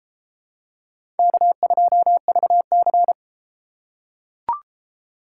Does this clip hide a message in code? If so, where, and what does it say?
Yes: Morse code from 1.2 to 3.1 s, heard as a keyed tone, about 30 wpm, 700 Hz. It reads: K2VC